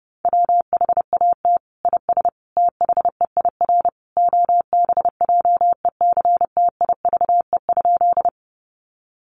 Morse code, heard as a keyed tone, about 30 wpm, 700 Hz. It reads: WHAT IS THEIR OBJECTIVE?
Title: Sentences from Top 1200 Words 30wpm